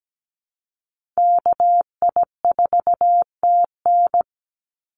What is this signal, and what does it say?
Morse code, heard as a keyed tone, about 17 wpm, 700 Hz. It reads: KI4TN
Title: NA SOTA Chaser Callsigns 17wpm